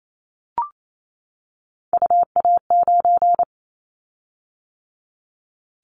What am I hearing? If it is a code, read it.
Morse code, heard as a keyed tone, about 28 wpm, 700 Hz. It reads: UA9
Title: Callsign Prefixes for Top 50 DXCC - 28wpm